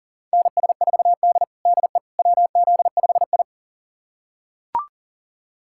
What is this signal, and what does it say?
Morse code, heard as a keyed tone, about 40 wpm, 700 Hz. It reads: NS4D DE WZ5I
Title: USA Callsign DE Callsign - 4 Characters - 40wpm